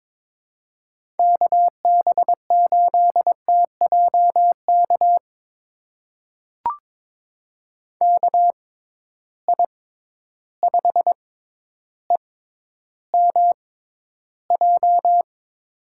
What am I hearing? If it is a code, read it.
Morse code, heard as a keyed tone, about 21 wpm, spaced as for 11 wpm, 700 Hz. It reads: KB8TJK KI5EMJ
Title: USA Call Signs - 6 Characters - 6x Character Spacing - 22wpm